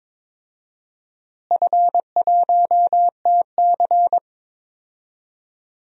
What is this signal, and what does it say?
Morse code, heard as a keyed tone, about 21 wpm, 700 Hz. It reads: F1TC
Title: DXCC-100 Call Signs - 4 Characters - Repeated 2x - 22wpm